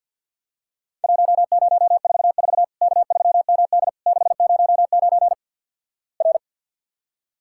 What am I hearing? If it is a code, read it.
Morse code, heard as a keyed tone, about 50 wpm, 700 Hz. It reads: 1044 K3MD 609 R
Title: Worked All Europe DX Contest - QTC Exchange - 50wpm